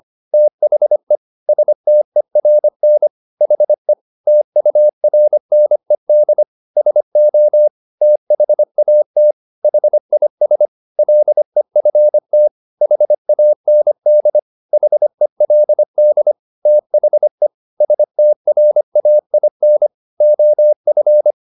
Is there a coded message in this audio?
Yes: Morse code from 0.3 to 21.3 s, heard as a keyed tone, about 25 wpm, 600 Hz. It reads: THE STERN HE TURNED SO THAT HIS LEFT HAND HELD THE STRAIN OF